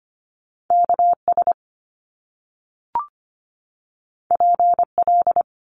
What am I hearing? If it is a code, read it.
Morse code, heard as a keyed tone, about 25 wpm, 700 Hz. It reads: KS PL
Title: Top Two Letter Combinations in Words 25wpm